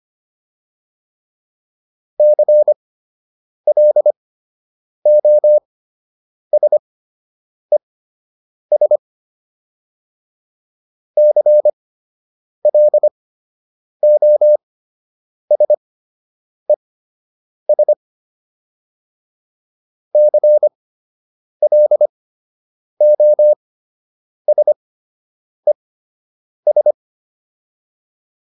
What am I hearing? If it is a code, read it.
Morse code, heard as a keyed tone, about 25 wpm, spaced as for 8 wpm, 600 Hz. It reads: CLOSES CLOSES CLOSES
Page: fundamental-practice-files